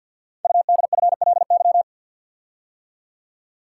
Morse code, heard as a keyed tone, about 50 wpm, 700 Hz. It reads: UDFLY
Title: Code Groups - 5 Letters - 50wpm